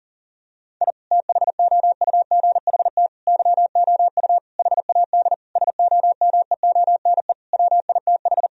Morse code, heard as a keyed tone, about 40 wpm, 700 Hz. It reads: I THOUGHT YOU HAD SOMEONE WITH
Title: Sentences from Top 1000 Words - Review Part 4 - 40wpm